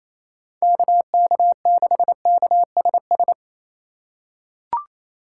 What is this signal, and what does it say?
Morse code, heard as a keyed tone, about 27 wpm, 700 Hz. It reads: KK6KSS
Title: USA Call Signs - 6 Characters - 4x Character Spacing - 28wpm